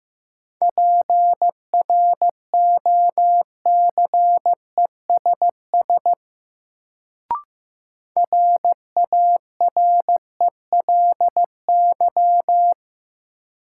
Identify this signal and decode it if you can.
Morse code, heard as a keyed tone, about 15 wpm, 700 Hz. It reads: PROCESS RARELY
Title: Sentences from Top 1400 Words - New Word Drill Alt - 15wpm